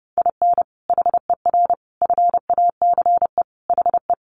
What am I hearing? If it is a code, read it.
Morse code, heard as a keyed tone, about 30 wpm, 700 Hz. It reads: IN HER FACE HE
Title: Sentences from Top 1300 Words 30wpm